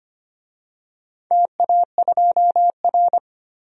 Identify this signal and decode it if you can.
Morse code, heard as a keyed tone, about 25 wpm, 700 Hz. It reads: TA2R
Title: DXCC-100 Call Signs - 4 Characters - 25wpm